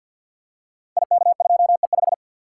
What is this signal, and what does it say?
Morse code, heard as a keyed tone, about 48 wpm, 690 Hz. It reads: IK2E5
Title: Code Groups - 5 Letter:Numbers - 50wpm